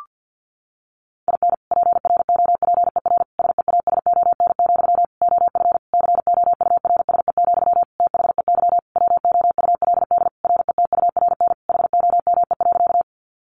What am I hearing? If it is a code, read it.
Morse code, heard as a keyed tone, about 50 wpm, 700 Hz. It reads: IN PROPER SEASON, OF COURSE, THEY WOULD RETURN HOME.
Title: Sentences from Top 1400 Words 50wpm